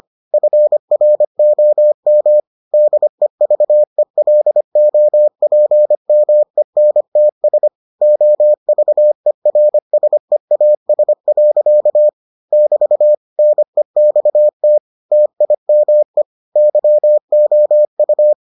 Morse code, heard as a keyed tone, about 25 wpm, 600 Hz. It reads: FROM DEVELOPMENTS OVERSEAS. <BT> NEXT TIME YOU